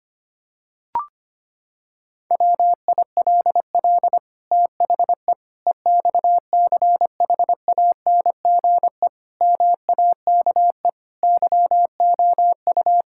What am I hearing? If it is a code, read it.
Morse code, heard as a keyed tone, about 25 wpm, 700 Hz. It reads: WILL THE EXCHANGE MAKE YOU
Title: Sentences from Top 1000 Words - Review Part 6 - 25wpm